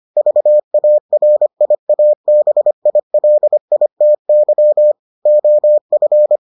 Morse code, heard as a keyed tone, about 25 wpm, 600 Hz. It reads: VARIABILITY OF